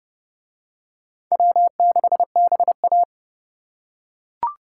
Morse code, heard as a keyed tone, about 29 wpm, 700 Hz. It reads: W6BA